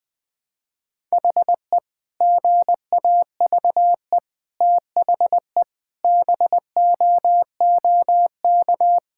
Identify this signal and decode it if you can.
Morse code, heard as a keyed tone, about 20 wpm, 700 Hz. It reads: HE GAVE THE BOOK
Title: Sentences with 4-Word Phrases - Spoken Beginning - 20wpm